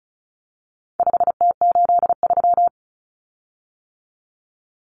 Morse code, heard as a keyed tone, about 34 wpm, 700 Hz. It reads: 5T83
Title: Common Cut Numbers - 4-Digit 35wpm